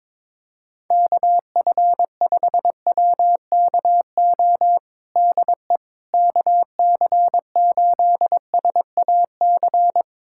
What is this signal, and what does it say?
Morse code, heard as a keyed tone, about 22 wpm, 700 Hz. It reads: KF5WKO DE KC8SAC
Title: USA Callsign DE Callsign 22wpm